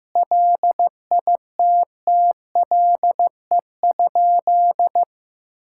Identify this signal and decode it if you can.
Morse code, heard as a keyed tone, about 15 wpm, 700 Hz. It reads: LITTLE?
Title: Sentences from Top 1600 Words 15wpm